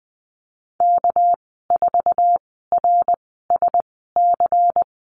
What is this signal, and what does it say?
Morse code, heard as a keyed tone, about 20 wpm, spaced as for 14 wpm, 700 Hz. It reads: K4RSC
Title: 231 Lesson 37g - Letter x - Call Signs 20wpm